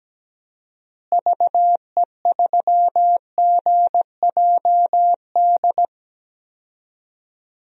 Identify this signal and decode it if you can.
Morse code, heard as a keyed tone, about 17 wpm, 700 Hz. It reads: VE3GJD
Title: DXCC-100 Call Signs - 6 Characters - Repeated 2x - 17wpm